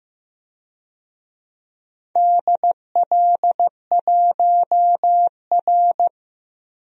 Morse code, heard as a keyed tone, about 15 wpm, 700 Hz. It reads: DL1R
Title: DXCC-100 Call Signs - 4 Characters - Repeated 3x - 15wpm